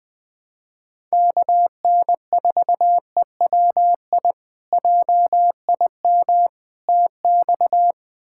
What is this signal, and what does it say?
Morse code, heard as a keyed tone, about 20 wpm, 700 Hz. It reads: KN4EWI JIM TX
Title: K1USN SST QSO Practice - 20wpm